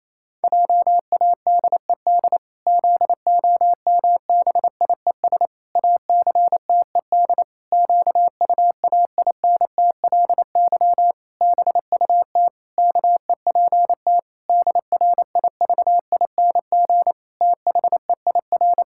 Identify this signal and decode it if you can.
Morse code, heard as a keyed tone, about 28 wpm, 700 Hz. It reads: JADED ZOMBIES ACTED QUAINTLY BUT KEPT DRIVING THEIR
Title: Pangram Challenge v2 - 28wpm